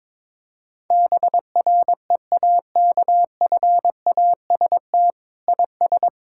Morse code, heard as a keyed tone, about 22 wpm, 700 Hz. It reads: BREAKFAST IS
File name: Sets of 9 Words - 22wpm